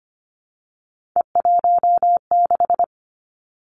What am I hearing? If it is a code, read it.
Morse code, heard as a keyed tone, about 25 wpm, 700 Hz. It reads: E16
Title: Three Challenging Letter-Number Combinations 25wpm